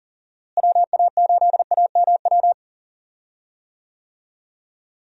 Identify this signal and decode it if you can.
Morse code, heard as a keyed tone, about 40 wpm, 700 Hz. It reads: WA8AMW
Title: USA Call Signs - 6 Characters - Repeated 3x - 40wpm